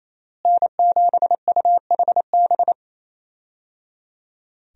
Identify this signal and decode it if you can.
Morse code, heard as a keyed tone, about 28 wpm, 700 Hz. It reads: N7UHB
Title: USA Call Signs - 5 Characters - Repeated 3x - 28wpm